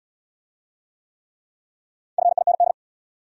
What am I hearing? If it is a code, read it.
Morse code, heard as a keyed tone, about 70 wpm, 700 Hz. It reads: FIND